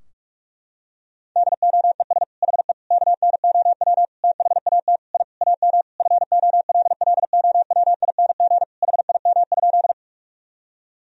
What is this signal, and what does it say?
Morse code, heard as a keyed tone, about 41 wpm, 700 Hz. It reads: DOES HE KNOW THAT I AM FOLLOWING HIM?
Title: Sentences from Top 1000 Words - Review Part 4 - 45wpm